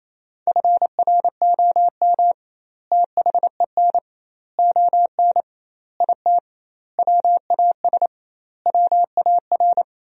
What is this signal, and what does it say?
Morse code, heard as a keyed tone, about 28 wpm, 700 Hz. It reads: FROM THEN ON IT WAS WAR
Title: 2x Word-Spacing - Sentences from Top 400 Words - 28wpm